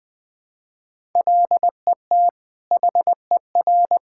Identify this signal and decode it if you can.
Morse code, heard as a keyed tone, about 20 wpm, 700 Hz. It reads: LET HER
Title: Sentences from Top 1000 Words - Review Part 6 - 20wpm